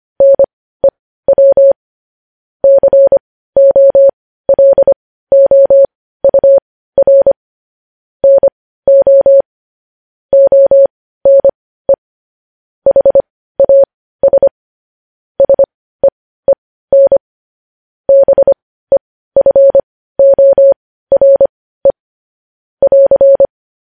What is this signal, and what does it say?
Morse code, heard as a keyed tone, about 25 wpm, spaced as for 15 wpm, 550 Hz. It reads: NEW COLOUR NO ONE HAS SEEN BEFORE <AR>
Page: news-headlines-in-morse-code-at-15-wpm